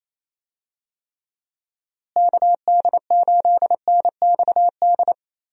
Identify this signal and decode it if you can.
Morse code, heard as a keyed tone, about 28 wpm, 700 Hz. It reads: KD8NXD
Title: USA Call Signs - 6 Characters - Repeated 3x - 28wpm